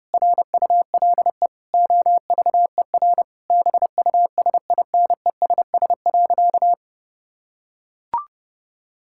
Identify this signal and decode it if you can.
Morse code, heard as a keyed tone, about 30 wpm, 700 Hz. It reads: RULE OVER BUSINESS.
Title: Sentences from Top 1000 Words - Review Part 5 - 30wpm